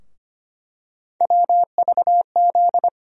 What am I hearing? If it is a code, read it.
Morse code, heard as a keyed tone, about 25 wpm, 700 Hz. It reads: WVZ
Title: USA Call Sign Postfixes - 3 Letters 25wpm